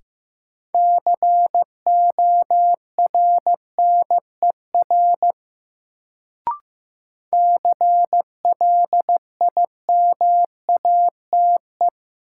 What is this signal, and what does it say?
Morse code, heard as a keyed tone, about 15 wpm, 700 Hz. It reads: CORNER CLIMATE
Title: Sentences from Top 1600 Words - New Word Drill - 15wpm